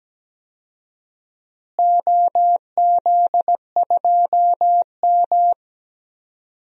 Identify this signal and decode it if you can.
Morse code, heard as a keyed tone, about 17 wpm, 700 Hz. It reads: OZ2M
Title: DXCC-100 Call Signs - 4 Characters - Repeated 2x - 17wpm